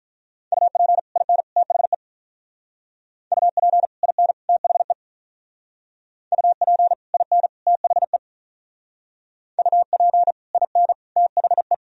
Morse code, heard as a keyed tone, about 45 wpm, 690 Hz. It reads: UP IN THE UP IN THE UP IN THE UP IN THE